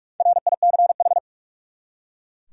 Morse code, heard as a keyed tone, about 41 wpm, 700 Hz. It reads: RICH
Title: USA Common Names - Max 5 Letters - 45wpm